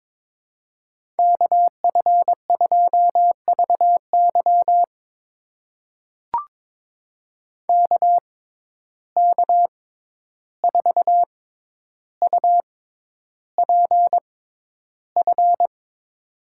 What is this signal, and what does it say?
Morse code, heard as a keyed tone, about 22 wpm, spaced as for 11 wpm, 700 Hz. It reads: KF2VY KK4UPF
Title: USA Call Signs - 6x Character Spacing - 22wpm